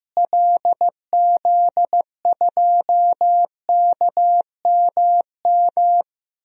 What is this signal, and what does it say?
Morse code, heard as a keyed tone, about 15 wpm, 700 Hz. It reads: LZ2KMM
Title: DXCC-100 Call Signs - 6 Characters - Repeated 2x - 15wpm